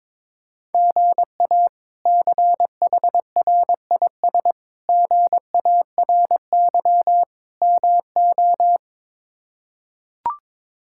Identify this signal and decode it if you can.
Morse code, heard as a keyed tone, about 21 wpm, 700 Hz. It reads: GA CHRIS GARY MO
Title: K1USN SST QSO Practice - 22wpm